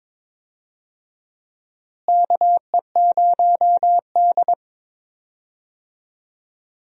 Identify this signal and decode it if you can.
Morse code, heard as a keyed tone, about 22 wpm, 700 Hz. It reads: KE0D
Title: USA Call Signs - 4 Characters - Repeated 3x - 22wpm